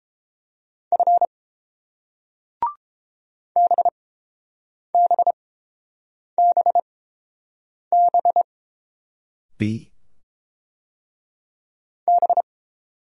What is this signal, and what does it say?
Morse code, heard as a keyed tone, about 30 wpm, spaced as for 7 wpm, 700 Hz. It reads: F BBBB B